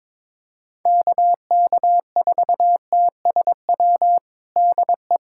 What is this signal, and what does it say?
Morse code, heard as a keyed tone, about 22 wpm, 700 Hz. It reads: KK4TSW DE
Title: USA Callsign DE Callsign - 6 Characters - 22wpm